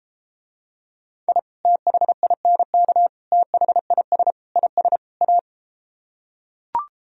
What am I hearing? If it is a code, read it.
Morse code, heard as a keyed tone, about 33 wpm, 700 Hz. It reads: I THINK THIS IS A